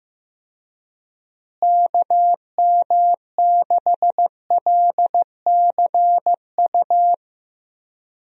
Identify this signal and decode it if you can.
Morse code, heard as a keyed tone, about 15 wpm, 700 Hz. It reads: KM6LCU
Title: USA Call Signs - 6 Characters - Repeated 3x - 15wpm